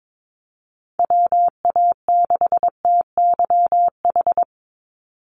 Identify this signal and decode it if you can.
Morse code, heard as a keyed tone, about 22 wpm, 700 Hz. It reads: WA6TYH
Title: USA Call Signs 22wpm